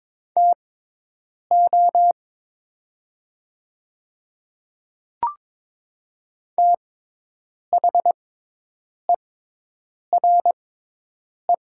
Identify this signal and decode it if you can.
Morse code, heard as a keyed tone, about 22 wpm, spaced as for 8 wpm, 700 Hz. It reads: TO THERE
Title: 6x Character Spacing - Top 100 Words - 22wpm